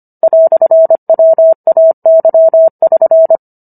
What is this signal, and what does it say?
Morse code, heard as a keyed tone, about 25 wpm, 650 Hz. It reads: "WAY<SN>